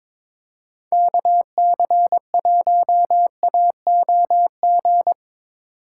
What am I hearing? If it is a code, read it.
Morse code, heard as a keyed tone, about 25 wpm, 700 Hz. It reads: KC1AOG
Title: Speed Racing - USA 2x3 Call Signs 22wpm